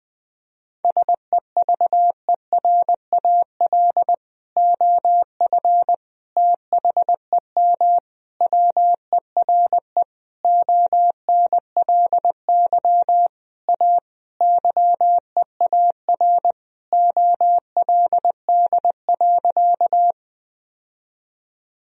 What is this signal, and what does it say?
Morse code, heard as a keyed tone, about 20 wpm, 700 Hz. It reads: SEVERAL OF THEM WERE ONLY A YEAR OLD.
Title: Sentences from Top 1000 Words - Review Part 2 - 20wpm